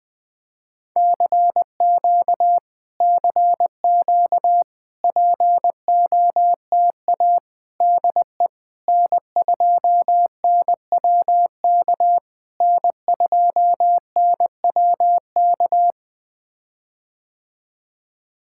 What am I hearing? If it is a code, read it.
Morse code, heard as a keyed tone, about 21 wpm, 700 Hz. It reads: CQ CQ POTA DE N2NWK N2NWK
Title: POTA QSO Practice - 20wpm